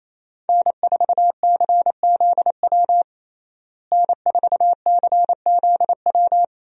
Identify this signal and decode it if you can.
Morse code, heard as a keyed tone, about 28 wpm, 700 Hz. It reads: N4CZW N4CZW
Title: USA Call Signs - 5 Characters - Repeated 2x - 28wpm